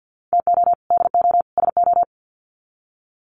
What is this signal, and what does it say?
Morse code, heard as a keyed tone, about 50 wpm, 700 Hz. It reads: TO DO SO
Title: Sets of 3 Words 50wpm